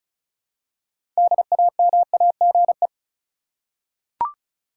Morse code, heard as a keyed tone, about 34 wpm, 700 Hz. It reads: DAMAGE
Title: Six Letter Words - 35wpm